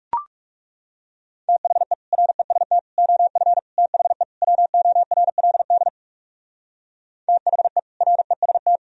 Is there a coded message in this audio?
Yes: Morse code from 1.5 to 8.8 s, heard as a keyed tone, about 38 wpm, 700 Hz. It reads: THE REST OF THE WORLD THE REST